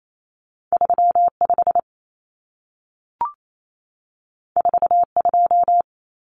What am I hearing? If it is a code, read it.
Morse code, heard as a keyed tone, about 28 wpm, 700 Hz. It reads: 35 42
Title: Common Cut Numbers - 2-Digit 28wpm